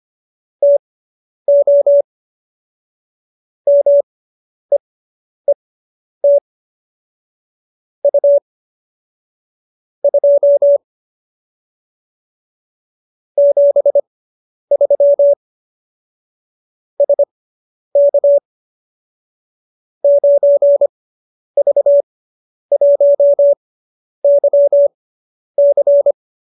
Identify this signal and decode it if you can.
Morse code, heard as a keyed tone, about 25 wpm, spaced as for 10 wpm, 570 Hz. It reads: TO MEET U 2 73 SK 9V1YC